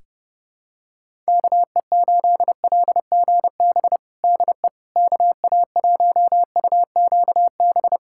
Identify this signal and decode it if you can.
Morse code, heard as a keyed tone, about 29 wpm, 700 Hz. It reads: KE8LGB DE KA1UQB
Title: USA Callsign DE Callsign - 6 Characters - 30wpm